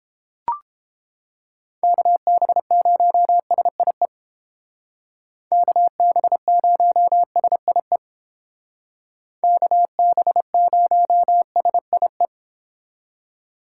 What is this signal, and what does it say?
Morse code, heard as a keyed tone, about 28 wpm, 700 Hz. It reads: KB0SIE KB0SIE KB0SIE